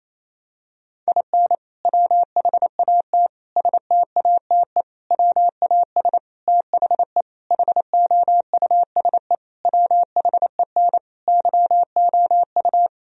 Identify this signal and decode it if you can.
Morse code, heard as a keyed tone, about 28 wpm, 700 Hz. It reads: IN WHAT STATE WAS THE HOUSE WHEN YOU
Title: Sentences from Top 1000 Words - Review Part 2 - 28wpm